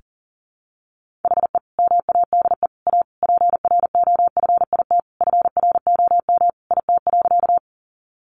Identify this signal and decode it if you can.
Morse code, heard as a keyed tone, about 40 wpm, 700 Hz. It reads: HE MADE A PROFIT FROM IT.
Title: Sentences from Top 800 Words 40wpm